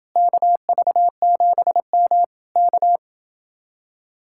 Morse code, heard as a keyed tone, about 29 wpm, 700 Hz. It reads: KV7M K